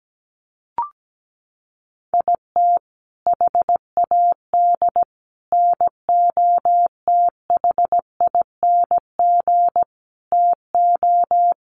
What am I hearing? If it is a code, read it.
Morse code, heard as a keyed tone, about 17 wpm, 700 Hz. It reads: IT HAD NOTHING TO
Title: Sets of 5 Words Encore2 17wpm